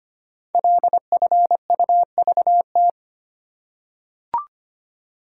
Morse code, heard as a keyed tone, about 25 wpm, 700 Hz. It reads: LFUVT
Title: Code Groups - 5 Letters - 25wpm